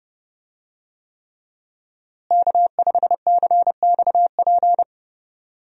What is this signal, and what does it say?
Morse code, heard as a keyed tone, about 30 wpm, 700 Hz. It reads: K5CXP
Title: USA Call Signs - 5 Characters - Repeated 3x - 30wpm